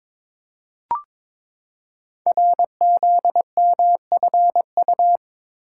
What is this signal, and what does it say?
Morse code, heard as a keyed tone, about 22 wpm, 700 Hz. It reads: RZMFU
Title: Code Groups - 5 Letter:Numbers - 22wpm